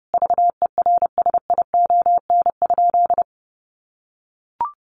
Morse code, heard as a keyed tone, about 30 wpm, 700 Hz. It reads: VERSION?
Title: Sentences from Top 1400 Words 30wpm